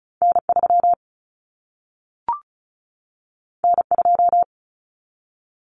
Morse code, heard as a keyed tone, about 36 wpm, 700 Hz. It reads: N3 N2
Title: Common Cut Numbers - 2-Digit 35wpm